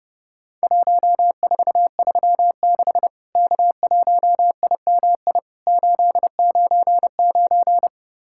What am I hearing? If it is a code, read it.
Morse code, heard as a keyed tone, about 32 wpm, 700 Hz. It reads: 1436 K1IMI 899
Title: Worked All Europe DX Contest - QTC Exchange - 30wpm